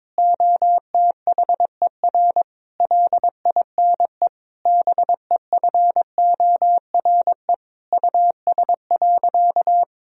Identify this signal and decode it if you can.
Morse code, heard as a keyed tone, about 22 wpm, 700 Hz. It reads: OTHER LINE BEFORE US.
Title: Sentences from Top 1000 Words - Review Part 1 - 22wpm